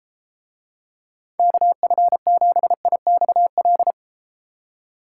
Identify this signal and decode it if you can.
Morse code, heard as a keyed tone, about 33 wpm, 700 Hz. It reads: KF7IXL